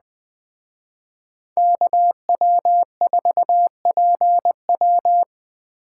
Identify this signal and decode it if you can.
Morse code, heard as a keyed tone, about 21 wpm, 700 Hz. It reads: KW4PW